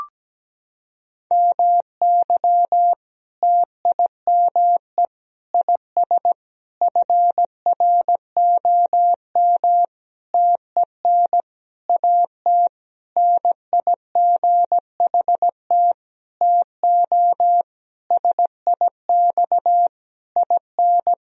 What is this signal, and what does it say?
Morse code, heard as a keyed tone, about 17 wpm, 700 Hz. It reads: MY TIME IS FROM TEN AT NIGHT TO SIX IN
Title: Sentences from Top 1000 Words - Review Part 2 - 17wpm